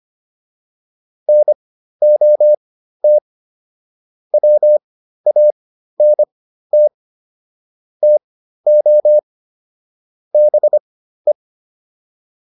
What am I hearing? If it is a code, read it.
Morse code, heard as a keyed tone, about 25 wpm, spaced as for 13 wpm, 600 Hz. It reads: NOT WANT TO BE
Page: /intermediate-practice-files/